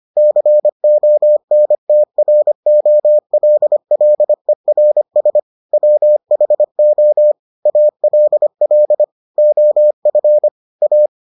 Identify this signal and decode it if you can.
Morse code, heard as a keyed tone, about 25 wpm, 600 Hz. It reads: CONTROLLERS WHO ALL OF A